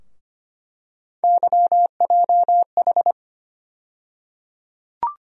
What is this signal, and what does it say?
Morse code, heard as a keyed tone, about 25 wpm, 700 Hz. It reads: YJH
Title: USA Call Sign Postfixes - 3 Letters 25wpm